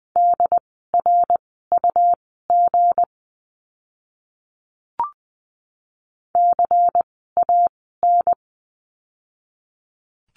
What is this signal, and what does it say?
Morse code, heard as a keyed tone, about 20 wpm, spaced as for 15 wpm, 700 Hz. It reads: DRUG CAN
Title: 160 Lesson 26f - Letter g - Words 20wpm